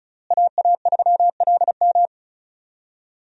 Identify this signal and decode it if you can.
Morse code, heard as a keyed tone, about 36 wpm, 700 Hz. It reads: AA3LM
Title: USA Call Signs - 5 Characters - Repeated 2x - 35wpm